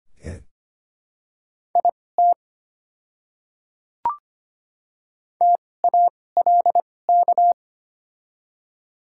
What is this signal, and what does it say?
Morse code, heard as a keyed tone, about 25 wpm, spaced as for 18 wpm, 690 Hz. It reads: IT TALK
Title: 2x Character Spacing - Top 500 Words - 25wpm